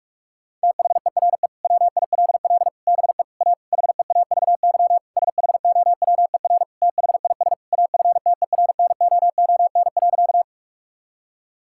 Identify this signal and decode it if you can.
Morse code, heard as a keyed tone, about 46 wpm, 700 Hz. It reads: THERE WILL BE A HEAVY SHOWER THIS AFTERNOON.
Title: Sentences from Top 1600 Words 45wpm